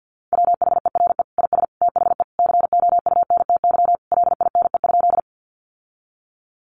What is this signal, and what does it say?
Morse code, heard as a keyed tone, about 50 wpm, 700 Hz. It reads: WHERE IS THE COUNTY LINE?
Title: Sentences from Top 1200 Words 50wpm